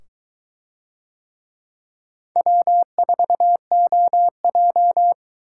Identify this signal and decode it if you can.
Morse code, heard as a keyed tone, about 22 wpm, 700 Hz. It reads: W4OJ